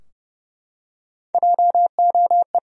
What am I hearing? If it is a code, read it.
Morse code, heard as a keyed tone, about 31 wpm, 700 Hz. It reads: JOE
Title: USA Common Names - Max 5 Letters - 30wpm